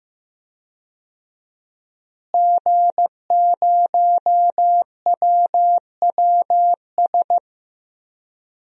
Morse code, heard as a keyed tone, about 15 wpm, 700 Hz. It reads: G0WWS
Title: DXCC-100 Call Signs - 5 Characters - Repeated 2x - 15wpm